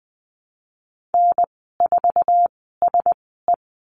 Morse code, heard as a keyed tone, about 20 wpm, spaced as for 15 wpm, 700 Hz. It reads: N4SE
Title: 055 Lesson 10g - Letter r - Call Signs 20wpm